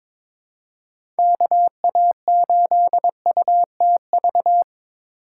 Morse code, heard as a keyed tone, about 22 wpm, 700 Hz. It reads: KA8UTV
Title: USA Call Signs - 6 Characters - 4x Character Spacing - 22wpm